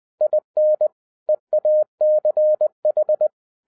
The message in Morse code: IN EACH

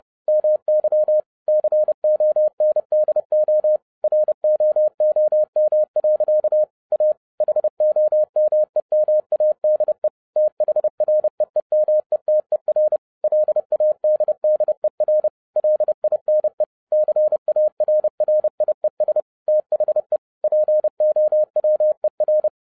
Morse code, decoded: MY CONDO ROOM. A HOMEMADE THREEMETER LADDER LINE CARRIES THE POWER

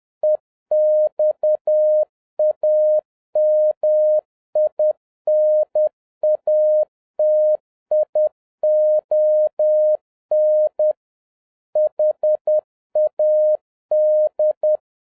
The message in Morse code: EXAMINATION HAD

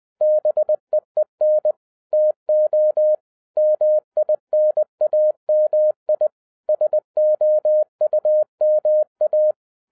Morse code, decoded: BEEN TO MINAMI SOUMA

